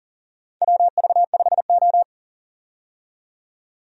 Morse code, heard as a keyed tone, about 37 wpm, 700 Hz. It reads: WV5O
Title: USA Call Signs - 4 Characters - 40wpm